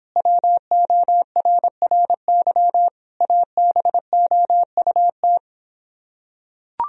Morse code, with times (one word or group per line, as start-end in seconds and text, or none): 0.2-2.9 WORRY
3.2-5.4 ABOUT